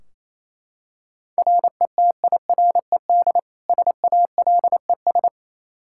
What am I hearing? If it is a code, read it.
Morse code, heard as a keyed tone, about 28 wpm, 700 Hz. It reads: RETIRED SALES
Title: Practice QSO - Exchange 3 - More Info - 28wpm